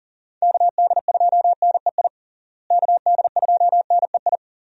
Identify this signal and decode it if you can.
Morse code, heard as a keyed tone, about 40 wpm, 700 Hz. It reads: KD2NEI KD2NEI
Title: USA Call Signs - 6 Characters - Repeated 2x - 40wpm